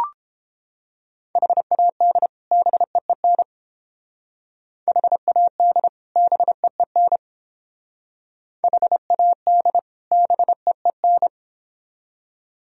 Morse code, heard as a keyed tone, about 29 wpm, 700 Hz. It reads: HAD BEEN HAD BEEN HAD BEEN